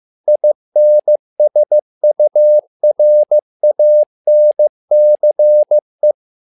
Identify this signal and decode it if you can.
Morse code, heard as a keyed tone, about 15 wpm, 600 Hz. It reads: INSURANCE
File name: learning center / tower trouble / tower trouble 15wpm